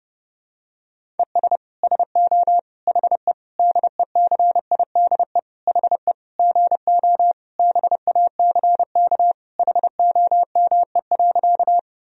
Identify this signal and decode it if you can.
Morse code, heard as a keyed tone, about 30 wpm, 700 Hz. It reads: ES SO HE DECIDE HE GO BACK HOME.
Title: Sentences from Top 500 Words and Abbreviations 30wpm